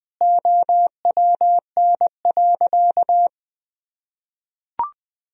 Morse code, heard as a keyed tone, about 20 wpm, 700 Hz. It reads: OWN.
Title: Sentences from Top 1000 Words - Review Part 2 - 20wpm